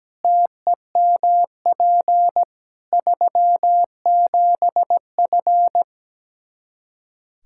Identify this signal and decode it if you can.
Morse code, heard as a keyed tone, about 17 wpm, 700 Hz. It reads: TEMP 37F
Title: K1USN SST QSO Practice 17wpm